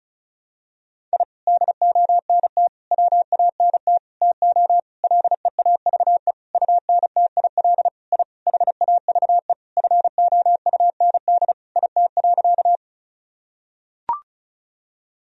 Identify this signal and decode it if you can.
Morse code, heard as a keyed tone, about 34 wpm, 700 Hz. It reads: I DONT WANT TO LEAVE UNTIL I HAVE FOUND IT.
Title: Sentences from Top 1000 Words - Review Part 1 - 35wpm